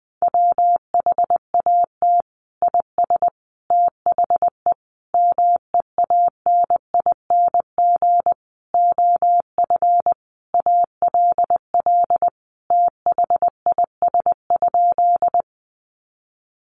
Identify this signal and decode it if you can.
Morse code, heard as a keyed tone, about 19 wpm, 700 Hz. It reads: WHAT IS THE MEANING OF ALL THIS?
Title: Sentences from Top 900 Words 20wpm